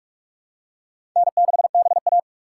Morse code, heard as a keyed tone, about 47 wpm, 700 Hz. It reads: N6BA
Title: USA Call Signs - 4 Characters - 6x Character Spacing - 45wpm